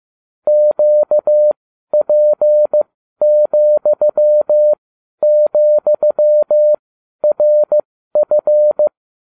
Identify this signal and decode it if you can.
Morse code, heard as a keyed tone, about 15 wpm, spaced as for 12 wpm, 600 Hz. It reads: QP,,RF